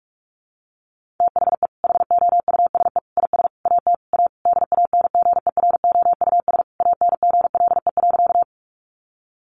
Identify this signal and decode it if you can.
Morse code, heard as a keyed tone, about 44 wpm, 700 Hz. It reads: THE HOUSE IS AT A DANGEROUS ANGLE.
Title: Sentences from Top 1500 Words 45wpm